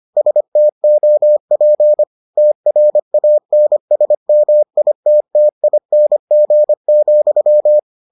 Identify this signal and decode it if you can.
Morse code, heard as a keyed tone, about 25 wpm, 600 Hz. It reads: STOP TRANSMITTING,